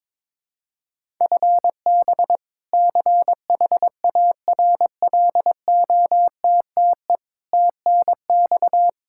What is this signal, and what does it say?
Morse code, heard as a keyed tone, about 22 wpm, 700 Hz. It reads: FB CHARLOTTE TNX
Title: Practice QSO - Exchange 3 - More Info - 22wpm